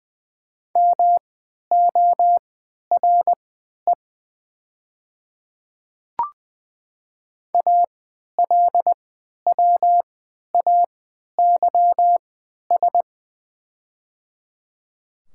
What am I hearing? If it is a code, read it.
Morse code, heard as a keyed tone, about 20 wpm, spaced as for 11 wpm, 700 Hz. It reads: MORE ALWAYS
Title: 3x Character Spacing - Top 500 Words - 20wpm